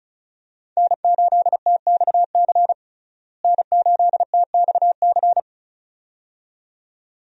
Morse code, heard as a keyed tone, about 36 wpm, 700 Hz. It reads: N8TXC N8TXC
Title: USA Call Signs - 5 Characters - Repeated 2x - 35wpm